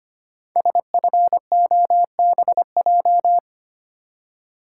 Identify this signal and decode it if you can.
Morse code, heard as a keyed tone, about 25 wpm, 700 Hz. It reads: SFOBJ
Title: Code Groups - 5 Letters - 25wpm